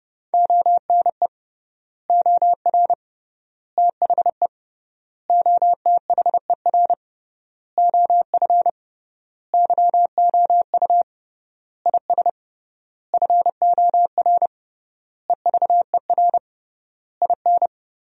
Morse code, heard as a keyed tone, about 31 wpm, 700 Hz. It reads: ONE OR THE OTHER OF YOU IS FOR EVER IN
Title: 3x Word-Spacing - Sentences from Top 400 Words - 30wpm